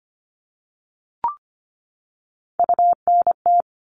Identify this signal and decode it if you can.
Morse code, heard as a keyed tone, about 25 wpm, 700 Hz. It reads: UNT
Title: All Cut Numbers - 3-Digit 25wpm